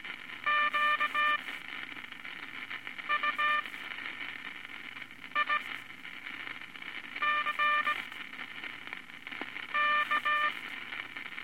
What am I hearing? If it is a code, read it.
Morse code, heard as a keyed tone, about 18 wpm, spaced as for 5 wpm, 1190 Hz, with radio noise. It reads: QUICK
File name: Morse Code Records/15A LCM 158 Units I-V Groups of Five